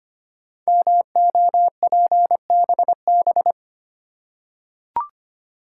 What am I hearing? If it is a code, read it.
Morse code, heard as a keyed tone, about 25 wpm, 700 Hz. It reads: MOPBB
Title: Code Groups - 5 Letters - 25wpm